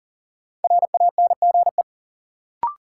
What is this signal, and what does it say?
Morse code, heard as a keyed tone, about 40 wpm, 700 Hz. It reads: RANGE
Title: Sets of 4 Words - New Word Drill - 40wpm